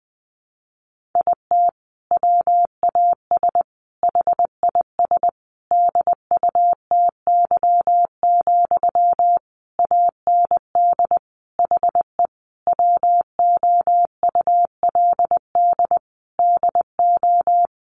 Morse code, heard as a keyed tone, about 20 wpm, 700 Hz. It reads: IT WAS HIS DUTY, AND HE WOULD DO
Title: Sentences from Top 1500 Words 20wpm